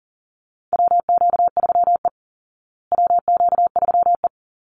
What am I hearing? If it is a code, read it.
Morse code, heard as a keyed tone, about 40 wpm, 700 Hz. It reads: WQ3E WQ3E
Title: Call Signs Repeated 2x - 40wpm